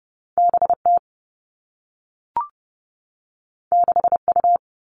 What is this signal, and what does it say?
Morse code, heard as a keyed tone, about 30 wpm, 700 Hz. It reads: B T 6 U
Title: Two Challenging Letter-Number Combinations 30wpm